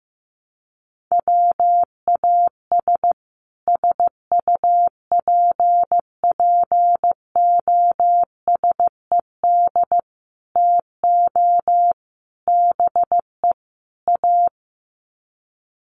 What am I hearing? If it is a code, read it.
Morse code, heard as a keyed tone, about 15 wpm, 700 Hz. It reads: WAS SUPPOSED TO BE A
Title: Sets of 5 Words Encore 15wpm